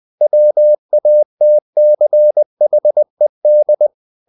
Morse code, heard as a keyed tone, about 20 wpm, 600 Hz. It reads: WATCHED
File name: the old man and the sea part 2 20wpm